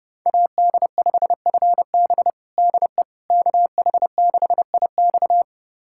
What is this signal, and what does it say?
Morse code, heard as a keyed tone, about 31 wpm, 700 Hz. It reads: AD5FB DE KH6IX
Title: USA Callsign DE Callsign - 5 Characters - 30wpm